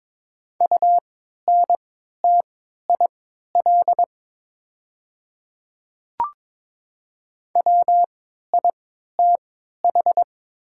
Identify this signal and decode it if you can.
Morse code, heard as a keyed tone, about 22 wpm, spaced as for 12 wpm, 700 Hz. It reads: UNTIL WITH
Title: 3x Character Spacing - Top 300 Words - 22wpm